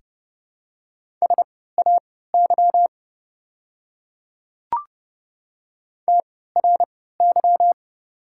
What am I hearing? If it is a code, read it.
Morse code, heard as a keyed tone, about 30 wpm, spaced as for 17 wpm, 690 Hz. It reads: SAY TRY
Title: 3x Character Spacing - Top 200 Words - 30wpm